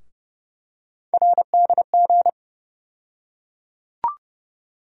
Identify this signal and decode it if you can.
Morse code, heard as a keyed tone, about 31 wpm, 700 Hz. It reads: RDG
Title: USA Call Sign Postfixes - 3 Letters 30wpm